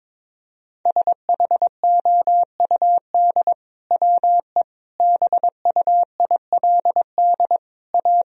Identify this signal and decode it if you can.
Morse code, heard as a keyed tone, about 22 wpm, 700 Hz. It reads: SHOUD WE BUILD A
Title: Sentences from Top 1000 Words - Review Part 4 - 22wpm